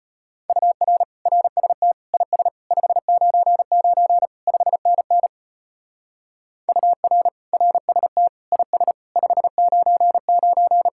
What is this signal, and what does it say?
Morse code, heard as a keyed tone, about 30 wpm, 700 Hz. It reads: UR RST IS 599 5NN UR RST IS 599
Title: Speed-Racing Common QSO Phrases - 25wpm